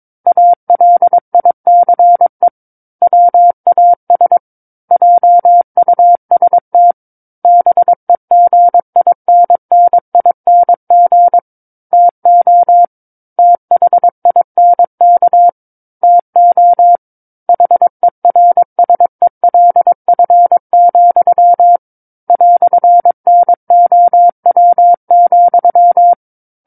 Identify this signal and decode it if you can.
Morse code, heard as a keyed tone, about 22 wpm, 700 Hz. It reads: ALICE WAS JUST BEGINNING TO THINK TO HERSELF, "NOW,
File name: 06-Chapter-06 22 WPM